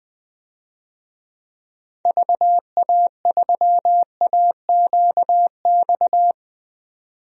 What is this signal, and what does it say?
Morse code, heard as a keyed tone, about 20 wpm, 700 Hz. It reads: VA3AQX
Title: DXCC-100 Call Signs - 6 Characters - 20wpm